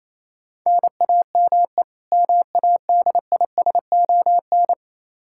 Morse code, heard as a keyed tone, about 27 wpm, 700 Hz. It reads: NAME MADISON
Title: Practice QSO - Exchange 1 - Beginning - 28wpm